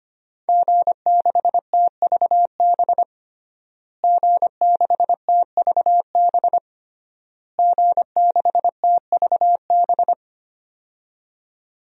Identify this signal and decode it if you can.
Morse code, heard as a keyed tone, about 25 wpm, 700 Hz. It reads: G6TVB G6TVB G6TVB
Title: DXCC-100 Call Signs - 5 Characters - Repeated 3x - 25wpm